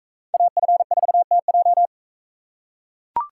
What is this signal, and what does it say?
Morse code, heard as a keyed tone, about 41 wpm, 700 Hz. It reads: AF4TJ